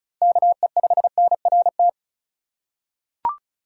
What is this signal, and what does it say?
Morse code, heard as a keyed tone, about 36 wpm, 700 Hz. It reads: KE5NRT
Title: USA Call Signs - 3x Character Spacing - 35wpm